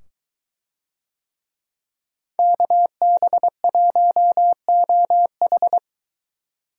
Morse code, heard as a keyed tone, about 21 wpm, 700 Hz. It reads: KB1OH